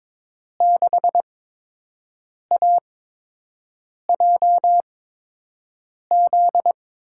Morse code, heard as a keyed tone, about 22 wpm, spaced as for 6 wpm, 700 Hz. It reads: 6AJZ
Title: USA Call Signs - 6 Characters - 8x Character Spacing - 22wpm